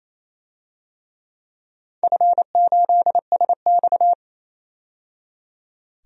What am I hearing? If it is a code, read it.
Morse code, heard as a keyed tone, about 28 wpm, 700 Hz. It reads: F8SX
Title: DXCC-100 Call Signs - 4 Characters - 28wpm